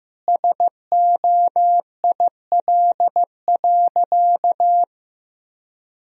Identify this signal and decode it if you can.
Morse code, heard as a keyed tone, about 15 wpm, 700 Hz. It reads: SOIL.
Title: Sentences from Top 1000 Words - Review Part 4 - 15wpm